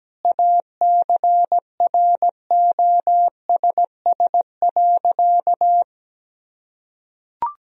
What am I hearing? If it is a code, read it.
Morse code, heard as a keyed tone, about 17 wpm, 700 Hz. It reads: ACROSS.
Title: Sentences from Top 1000 Words - Review Part 3 - 17wpm